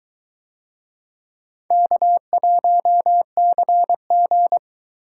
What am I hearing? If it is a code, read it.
Morse code, heard as a keyed tone, about 21 wpm, 700 Hz. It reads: K1CG